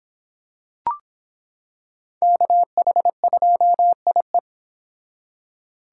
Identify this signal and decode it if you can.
Morse code, heard as a keyed tone, about 26 wpm, 700 Hz. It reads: KH2IE